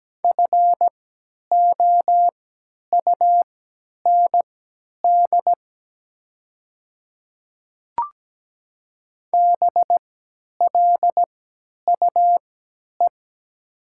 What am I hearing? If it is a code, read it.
Morse code, heard as a keyed tone, about 17 wpm, spaced as for 10 wpm, 700 Hz. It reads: FOUND BLUE
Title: 3x Character Spacing - Top 500 Words - 17wpm